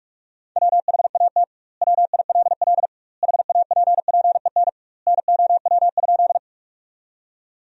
Morse code, heard as a keyed tone, about 46 wpm, 700 Hz. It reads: WHAT WILL HAPPEN NOW?
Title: 2x Word-Spacing - Sentences from Top 400 Words - 45wpm